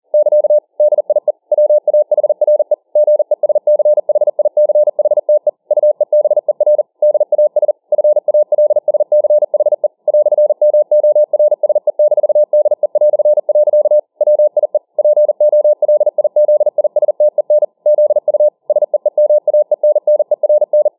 Morse code, heard as a keyed tone, about 40 wpm, 600 Hz. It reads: <KA> DIE WAHRE GESCHICHTE UEBER DAS FALSCHE "MORSE-DE". WIE POLIZISTEN ZU SEEMAENNERN